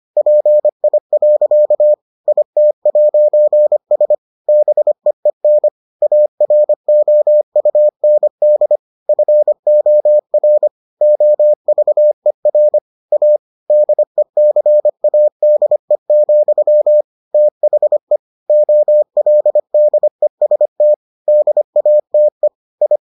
Morse code, heard as a keyed tone, about 25 wpm, 600 Hz. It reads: PI. IT'S BEEN AROUND FOR OVER A DECADE, THE OLDEST DATE I